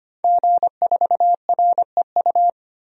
Morse code, heard as a keyed tone, about 25 wpm, 700 Hz. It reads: G4REU
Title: DXCC-100 Call Signs - 5 Characters - 25wpm